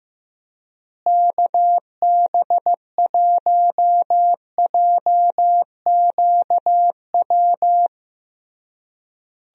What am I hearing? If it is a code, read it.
Morse code, heard as a keyed tone, about 15 wpm, 700 Hz. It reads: KB1JQW
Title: USA Call Signs - 6 Characters - Repeated 3x - 15wpm